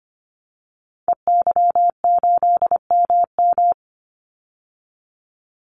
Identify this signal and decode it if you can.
Morse code, heard as a keyed tone, about 25 wpm, 700 Hz. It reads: EY8MM
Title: Call Signs Repeated 3x - 25wpm